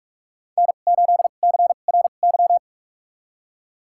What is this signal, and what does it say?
Morse code, heard as a keyed tone, about 46 wpm, spaced as for 32 wpm, 700 Hz. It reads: N8CRY
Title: USA Call Signs - 5 Characters - 2x Character Spacing - 45wpm